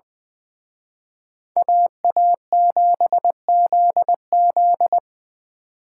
Morse code, heard as a keyed tone, about 25 wpm, 700 Hz. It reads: AA7ZZ
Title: Speed Racing - USA 2x2 Call Signs 20wpm